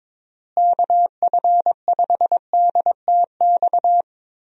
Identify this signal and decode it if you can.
Morse code, heard as a keyed tone, about 22 wpm, 700 Hz. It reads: KF5DTX
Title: USA Call Signs - 6 Characters - Repeated 3x - 22wpm